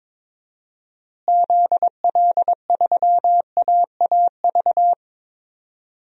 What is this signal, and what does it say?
Morse code, heard as a keyed tone, about 21 wpm, 700 Hz. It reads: ZL3AAV
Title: DXCC-100 Call Signs - 6 Characters - Repeated 2x - 22wpm